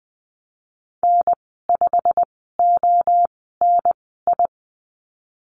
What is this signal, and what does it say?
Morse code, heard as a keyed tone, about 20 wpm, spaced as for 14 wpm, 700 Hz. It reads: N5ONI
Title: 161 Lesson 26g - Letter g - Call Signs 20wpm